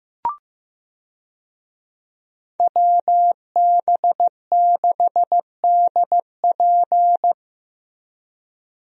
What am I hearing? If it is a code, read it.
Morse code, heard as a keyed tone, about 15 wpm, 700 Hz. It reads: WB6DP
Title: USA Call Signs - 5 Characters - Repeated 2x - 15wpm